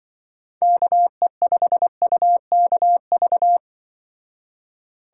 Morse code, heard as a keyed tone, about 24 wpm, 700 Hz. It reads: KE5UKV
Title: Speed Racing - USA 2x3 Call Signs 20wpm